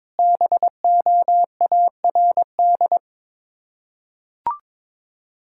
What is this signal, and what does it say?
Morse code, heard as a keyed tone, about 22 wpm, 700 Hz. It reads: BOARD
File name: practice / 22wpm / Sets of 7 Words - 22wpm